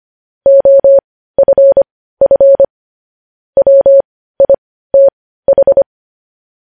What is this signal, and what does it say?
Morse code, heard as a keyed tone, about 25 wpm, spaced as for 15 wpm, 550 Hz. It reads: OFF WITH
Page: News Headlines in Morse Code at 15 WPM